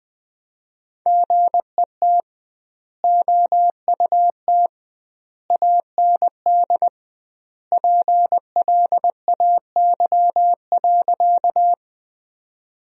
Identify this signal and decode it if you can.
Morse code, heard as a keyed tone, about 20 wpm, 700 Hz. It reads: GET OUT AND PLAY.
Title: 2x Word-Spacing - Sentences from Top 200 Words - 20wpm